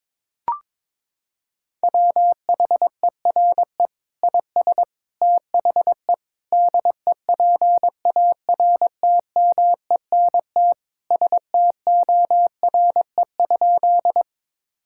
Morse code, heard as a keyed tone, about 22 wpm, 700 Hz. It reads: WHERE IS THE DEPARTMENT STORE?
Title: Sentences from Top 1000 Words - Review Part 5 - 22wpm